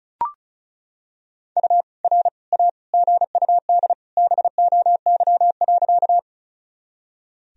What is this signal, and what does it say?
Morse code, heard as a keyed tone, about 34 wpm, 690 Hz. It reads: U R A GUD BOY.
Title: Sentences from Top 200 Words and Abbreviations 35wpm